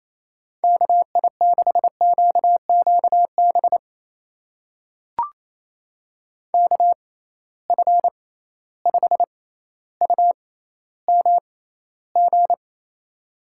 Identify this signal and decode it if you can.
Morse code, heard as a keyed tone, about 28 wpm, spaced as for 14 wpm, 700 Hz. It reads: KI6QQB KF5UMG
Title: USA Call Signs - 6 Characters - 6x Character Spacing - 28wpm